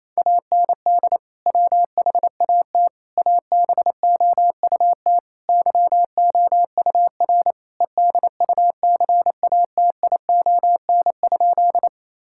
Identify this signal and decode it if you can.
Morse code, heard as a keyed tone, about 28 wpm, 700 Hz. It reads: AND WHAT ABOUT YOUR EDUCATION?
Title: Sentences from Top 1000 Words - Review Part 6 - 28wpm